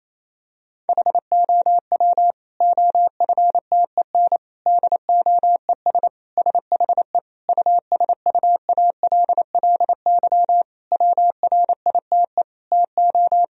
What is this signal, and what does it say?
Morse code, heard as a keyed tone, about 27 wpm, 700 Hz. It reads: HOW OFTEN DOES SHE USUALLY WRITE TO
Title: Sentences from Top 1000 Words - Review Part 3 - 28wpm